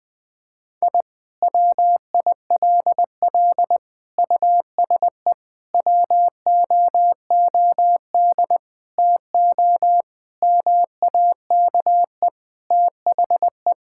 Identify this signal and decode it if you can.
Morse code, heard as a keyed tone, about 20 wpm, 700 Hz. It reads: I WILL USE WOOD TO MAKE THE
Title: Sentences from Top 1000 Words - Review Part 6 - 20wpm